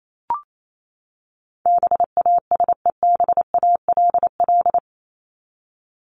Morse code, heard as a keyed tone, about 28 wpm, 700 Hz. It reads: BASEBALL
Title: Top 1751-2000 Words 28wpm